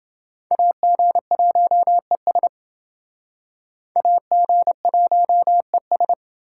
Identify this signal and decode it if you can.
Morse code, heard as a keyed tone, about 29 wpm, 700 Hz. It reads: AG1ES AG1ES